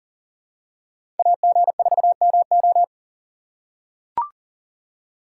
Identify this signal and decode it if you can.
Morse code, heard as a keyed tone, about 40 wpm, 700 Hz. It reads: AG4MO
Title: USA Call Signs - 5 Characters - 3x Character Spacing - 40wpm